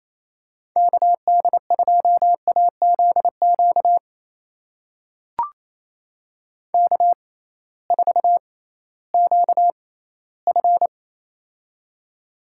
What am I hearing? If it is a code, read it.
Morse code, heard as a keyed tone, about 27 wpm, spaced as for 16 wpm, 700 Hz. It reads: KD2AZQ K4QF
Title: USA Call Signs - 6x Character Spacing - 28wpm